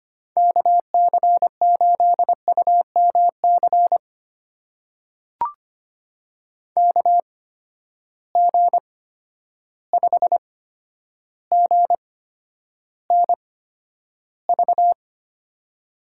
Morse code, heard as a keyed tone, about 25 wpm, spaced as for 11 wpm, 700 Hz. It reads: KC8UMC KG5GNV
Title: USA Call Signs - 6 Characters - 8x Character Spacing - 25wpm